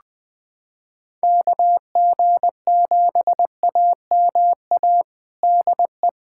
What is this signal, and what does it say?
Morse code, heard as a keyed tone, about 20 wpm, 700 Hz. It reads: KG7AMA DE
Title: USA Callsign DE Callsign 20wpm